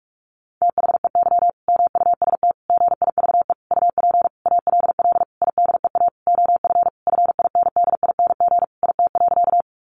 Morse code, heard as a keyed tone, about 44 wpm, 700 Hz. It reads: THEY MUST GIVE UP ALL IDEA OF FINDING IT.
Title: Sentences from Top 1100 Words 45wpm